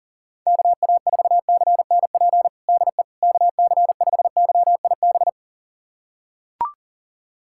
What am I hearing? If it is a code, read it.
Morse code, heard as a keyed tone, about 40 wpm, 700 Hz. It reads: KA4CNP DE KC5YIB
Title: USA Callsign DE Callsign - 6 Characters - 40wpm